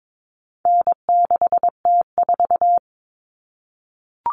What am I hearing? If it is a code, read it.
Morse code, heard as a keyed tone, about 22 wpm, 700 Hz. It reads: N6T4
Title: All Cut Numbers - 4-Digit 22wpm